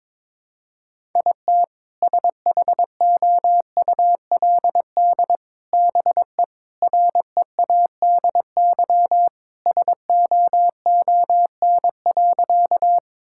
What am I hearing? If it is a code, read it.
Morse code, heard as a keyed tone, about 21 wpm, 700 Hz. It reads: IT SHOULD BE READY SOON.
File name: practice / 22wpm / Sentences with 2-Word Phrases - 22wpm